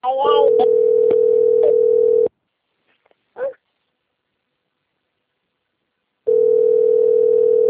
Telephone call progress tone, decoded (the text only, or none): ringback tone